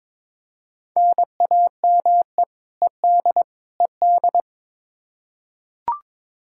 Morse code, heard as a keyed tone, about 22 wpm, 700 Hz. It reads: NAME ED ED
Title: Practice QSO - Exchange 1 - Beginning - 22wpm